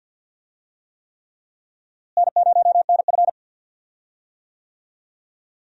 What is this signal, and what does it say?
Morse code, heard as a keyed tone, about 47 wpm, 700 Hz. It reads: N0NF
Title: USA Call Signs - 4 Characters - Repeated 3x - 50wpm